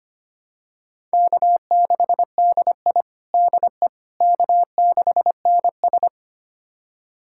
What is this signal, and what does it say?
Morse code, heard as a keyed tone, about 25 wpm, 700 Hz. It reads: K6DI DE K6NS
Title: USA Callsign DE Callsign - 4 Characters - 25wpm